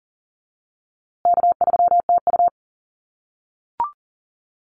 Morse code, heard as a keyed tone, about 40 wpm, 700 Hz. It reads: K3TU
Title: USA Call Signs Repeated 3x 40wpm